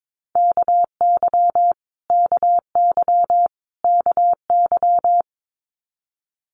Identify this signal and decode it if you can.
Morse code, heard as a keyed tone, about 22 wpm, 700 Hz. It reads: KY KY KY
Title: US States Repeated 3x 22wpm